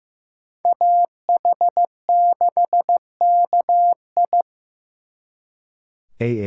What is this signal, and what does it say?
Morse code, heard as a keyed tone, about 18 wpm, 700 Hz. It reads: AH6KI
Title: Speed Racing - USA 2x2 Call Signs 15wpm